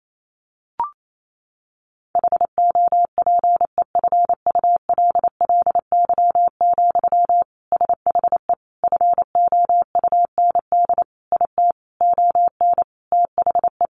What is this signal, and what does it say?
Morse code, heard as a keyed tone, about 28 wpm, 700 Hz. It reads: HOPEFULLY, SHE FOUND IT ON THE
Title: Sentences from Top 1500 Words 28wpm